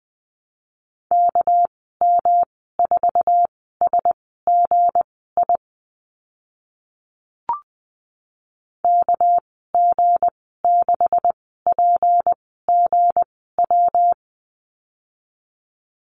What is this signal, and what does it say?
Morse code, heard as a keyed tone, about 20 wpm, spaced as for 14 wpm, 700 Hz. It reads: KM4SGI KG6PGW
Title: 245 Lesson 39g - Letter z - Call Signs 20wpm